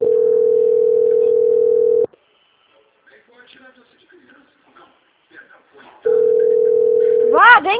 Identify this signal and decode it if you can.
Ringback tone (North America), 440 + 480 Hz, with speech.
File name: out- - - - - .67926